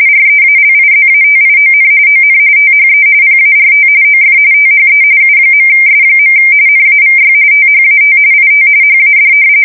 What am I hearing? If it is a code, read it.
RTTY, about 45.45 baud. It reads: REALIZATION ON THE PART OF THE U.S. NAVY THAT IT HAD
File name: fsk170 / usnavy3